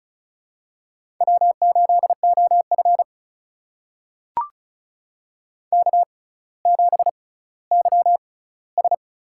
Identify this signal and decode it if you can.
Morse code, heard as a keyed tone, about 36 wpm, spaced as for 18 wpm, 700 Hz. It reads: W8OF K7YS
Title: USA Call Signs - 4 Characters - 6x Character Spacing - 35wpm